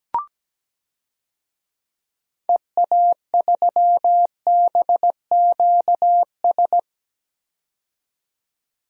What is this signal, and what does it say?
Morse code, heard as a keyed tone, about 17 wpm, 700 Hz. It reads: EA3BQS
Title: DXCC-100 Call Signs - 6 Characters - 17wpm